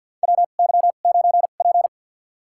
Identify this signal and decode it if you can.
Morse code, heard as a keyed tone, about 52 wpm, spaced as for 37 wpm, 700 Hz. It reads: WY9P
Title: USA Call Signs - 4 Characters - 2x Character Spacing - 50wpm